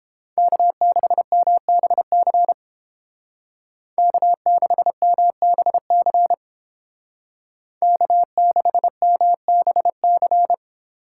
Morse code, heard as a keyed tone, about 29 wpm, 700 Hz. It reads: K6MBC K6MBC K6MBC